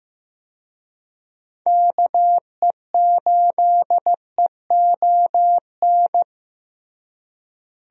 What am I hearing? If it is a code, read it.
Morse code, heard as a keyed tone, about 15 wpm, 700 Hz. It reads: KE8EON
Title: USA Call Signs - 6 Characters - Repeated 2x - 15wpm